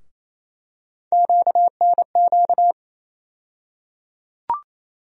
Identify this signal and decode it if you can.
Morse code, heard as a keyed tone, about 28 wpm, 700 Hz. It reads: QNQ
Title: USA Call Sign Postfixes - 3 Letters 28wpm